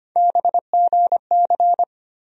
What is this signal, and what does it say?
Morse code, heard as a keyed tone, about 25 wpm, 700 Hz. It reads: BGC
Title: USA Call Sign Postfixes - 3 Letters 25wpm